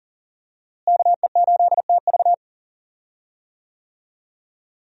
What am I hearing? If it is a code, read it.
Morse code, heard as a keyed tone, about 37 wpm, 700 Hz. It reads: KE8TV
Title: USA Call Signs - 5 Characters - Repeated 2x - 40wpm